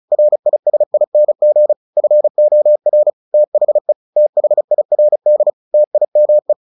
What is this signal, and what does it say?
Morse code, heard as a keyed tone, about 34 wpm, 600 Hz. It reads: RISING FOR THE THIRD TIME
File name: the old man and the sea 2 35wpm